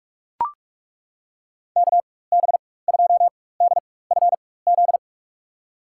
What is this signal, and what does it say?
Morse code, heard as a keyed tone, about 43 wpm, spaced as for 21 wpm, 700 Hz. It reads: KB2DFZ
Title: USA Call Signs - 6 Characters - 4x Character Spacing - 45wpm